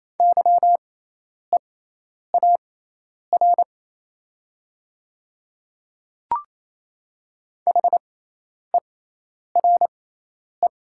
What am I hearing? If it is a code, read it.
Morse code, heard as a keyed tone, about 28 wpm, spaced as for 10 wpm, 700 Hz. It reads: YEAR HERE
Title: 6x Character Spacing - Top 200 Words - 28wpm